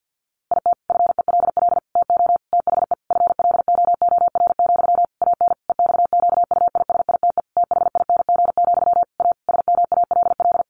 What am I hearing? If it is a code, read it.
Morse code, heard as a keyed tone, about 50 wpm, 700 Hz. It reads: IT FELL TO THE FLOOR, AN EXQUISITE THING, A SMALL
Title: A Sound Of Thunder 50wpm